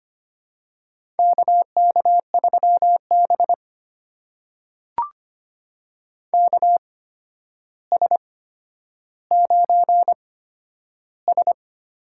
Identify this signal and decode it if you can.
Morse code, heard as a keyed tone, about 25 wpm, spaced as for 11 wpm, 700 Hz. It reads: KK3B KS9S
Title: USA Call Signs - 4 Characters - 8x Character Spacing - 25wpm